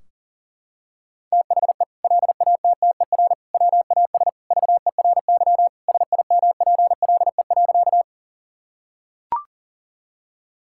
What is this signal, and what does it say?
Morse code, heard as a keyed tone, about 37 wpm, 700 Hz. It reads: THE LATTER WAS VERY SIMPLE.
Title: Sentences from Top 1600 Words 40wpm